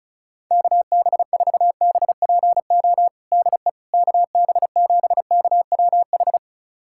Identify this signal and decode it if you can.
Morse code, heard as a keyed tone, about 34 wpm, 690 Hz. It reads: KB4BPO DE KB7KWH
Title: USA Callsign DE Callsign - 6 Characters - 35wpm